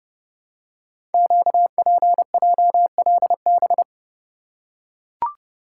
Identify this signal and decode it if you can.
Morse code, heard as a keyed tone, about 30 wpm, 700 Hz. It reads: QPJLB
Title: Code Groups - 5 Letters - 30wpm